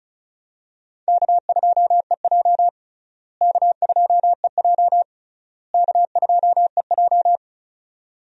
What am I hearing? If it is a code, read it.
Morse code, heard as a keyed tone, about 36 wpm, 700 Hz. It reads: K2EJ K2EJ K2EJ
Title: USA Call Signs - 4 Characters - Repeated 3x - 35wpm